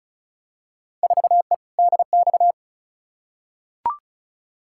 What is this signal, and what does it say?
Morse code, heard as a keyed tone, about 34 wpm, 700 Hz. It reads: 4E DX
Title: Field Day Exchanges Encore - 35wpm